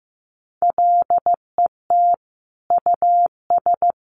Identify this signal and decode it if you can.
Morse code, heard as a keyed tone, about 15 wpm, 700 Hz. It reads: LET US
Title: Sentences from Top 1500 Words 15wpm